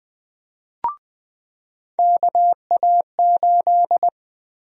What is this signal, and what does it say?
Morse code, heard as a keyed tone, about 20 wpm, 700 Hz. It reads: KA8
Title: USA Call Sign Prefixes - 20wpm